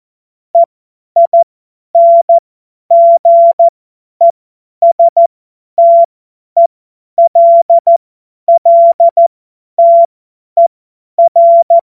Morse code, heard as a keyed tone, about 14 wpm, spaced as for 10 wpm, 680 Hz. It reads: EINGESTELLTER
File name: Morsekurs / Witze / Witz 30. Dezember 2025 50 F70